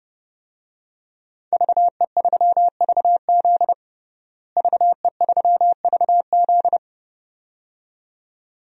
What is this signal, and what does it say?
Morse code, heard as a keyed tone, about 32 wpm, 700 Hz. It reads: VE3VZ VE3VZ
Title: DXCC-100 Call Signs - 5 Characters - Repeated 2x - 30wpm